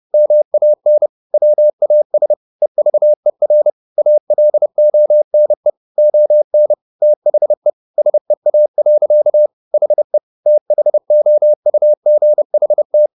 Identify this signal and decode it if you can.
Morse code, heard as a keyed tone, about 30 wpm, 600 Hz. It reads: MAN WAS EVER ALONE ON THE SEA. HE THOUGHT
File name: the old man and the sea / the old man and the sea 1 30wpm